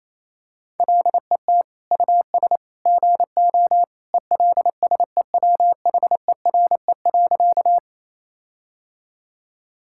Morse code, heard as a keyed tone, about 28 wpm, 700 Hz. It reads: LET US GO ELSEWHERE.
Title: Sentences from Top 1600 Words 28wpm